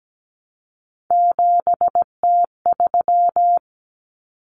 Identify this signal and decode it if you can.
Morse code, heard as a keyed tone, about 17 wpm, 700 Hz. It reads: 7T3
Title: Common Cut Numbers - 3-Digit 17wpm